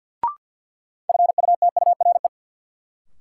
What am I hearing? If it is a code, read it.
Morse code, heard as a keyed tone, about 51 wpm, 700 Hz. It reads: FUTURE
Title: Sentences from Top 700 Words - New Word Drill - 50wpm